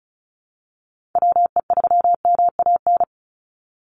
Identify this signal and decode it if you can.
Morse code, heard as a keyed tone, about 34 wpm, 700 Hz. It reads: WE3MAN
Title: USA Call Signs 35wpm